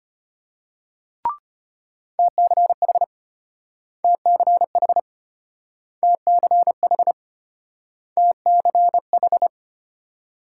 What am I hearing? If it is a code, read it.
Morse code, heard as a keyed tone, about 34 wpm, 700 Hz. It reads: TCH TCH TCH TCH